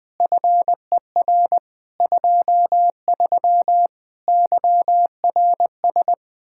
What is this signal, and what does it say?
Morse code, heard as a keyed tone, about 21 wpm, 700 Hz. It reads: FER 23 YRS
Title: K1USN SST QSO Practice 20wpm